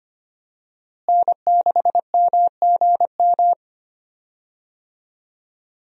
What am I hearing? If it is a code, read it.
Morse code, heard as a keyed tone, about 25 wpm, 700 Hz. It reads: N6MGM
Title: USA Call Signs - 5 Characters - Repeated 3x - 25wpm